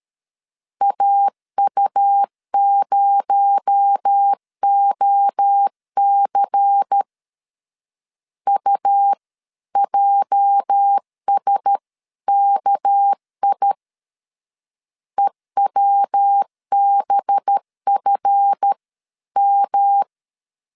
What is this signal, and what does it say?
Morse code, heard as a keyed tone, about 13 wpm, 800 Hz. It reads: AU0OC UJSKI EWBFM
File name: LEZIONI / 8 YJ45 / 8 YJ 12wpm 2235